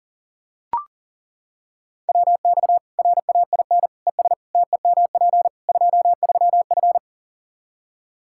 Morse code, heard as a keyed tone, about 40 wpm, 700 Hz. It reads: WX RAIN ES TEMP 23F
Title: K1USN SST QSO Practice 40wpm